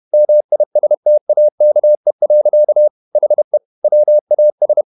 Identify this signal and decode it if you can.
Morse code, heard as a keyed tone, about 31 wpm, 600 Hz. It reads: MISTAKE. HE WAS
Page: Never 31wpm | CW med Gnister